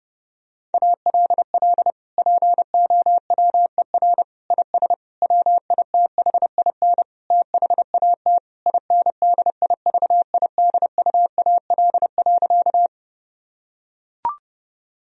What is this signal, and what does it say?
Morse code, heard as a keyed tone, about 29 wpm, 700 Hz. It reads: ALL POWER IS WITHIN THAT INDIVIDUAL.
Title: Sentences from Top 1000 Words - Review Part 4 - 30wpm